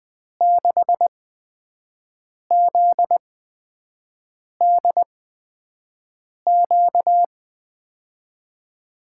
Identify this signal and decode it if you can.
Morse code, heard as a keyed tone, about 20 wpm, spaced as for 5 wpm, 700 Hz. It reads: 6ZDQ
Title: USA Call Signs - 6 Characters - 8x Character Spacing - 20wpm